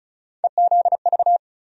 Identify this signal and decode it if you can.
Morse code, heard as a keyed tone, about 36 wpm, 700 Hz. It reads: EZV
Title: USA Call Sign Postfixes - 3 Letters 35wpm